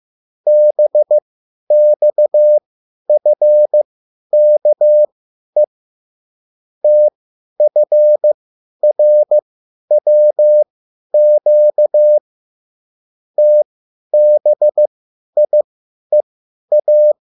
Morse code, heard as a keyed tone, about 15 wpm, spaced as for 10 wpm, 600 Hz. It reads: BXFKE TFRWQ TBIEA